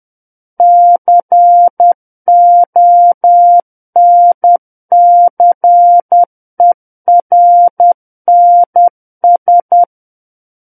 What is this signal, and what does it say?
Morse code, heard as a keyed tone, about 10 wpm, 700 Hz. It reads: CONCERNS
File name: Book One / 01-Chapter 1 - 10 WPM